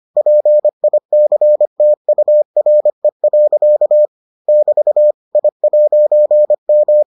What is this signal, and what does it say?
Morse code, heard as a keyed tone, about 25 wpm, 600 Hz. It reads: PICTURE. <BT> I'M